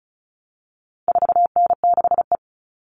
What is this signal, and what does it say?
Morse code, heard as a keyed tone, about 34 wpm, 700 Hz. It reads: 4N6E
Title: All Cut Numbers - 4-Digit 35wpm